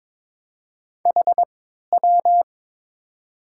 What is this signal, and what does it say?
Morse code, heard as a keyed tone, about 22 wpm, spaced as for 13 wpm, 700 Hz. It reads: HW
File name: Top 90 QSO Elements - 3x Character Spacing - 22wpm